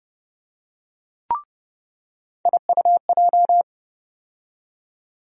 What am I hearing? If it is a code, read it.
Morse code, heard as a keyed tone, about 30 wpm, 700 Hz. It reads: IUJ
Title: USA Call Sign Postfixes - 3 Letters 30wpm